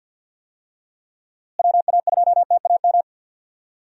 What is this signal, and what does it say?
Morse code, heard as a keyed tone, about 46 wpm, 700 Hz. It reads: WA2TAM
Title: USA Call Signs - 6 Characters - Repeated 3x - 50wpm